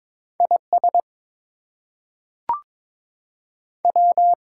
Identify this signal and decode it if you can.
Morse code, heard as a keyed tone, about 22 wpm, 700 Hz. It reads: IS W
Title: Two Letter Words - 22wpm